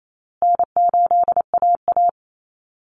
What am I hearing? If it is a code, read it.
Morse code, heard as a keyed tone, about 28 wpm, 700 Hz. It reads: N8AA
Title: Easy Call Signs Repeated 3x - 28wpm